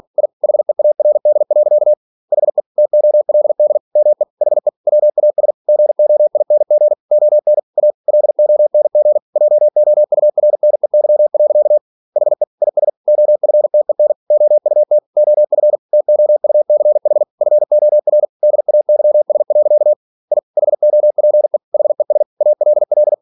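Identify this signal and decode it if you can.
Morse code, heard as a keyed tone, about 46 wpm, 600 Hz. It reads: I HEARD. HE TOLD ME HE WAS GOING ON A LONG JOURNEY. HE IS OFTEN OUT OF TOUCH FOR DAYS. I HOPE HES ALL
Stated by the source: Practice text from an English novel sent at 47 WPM.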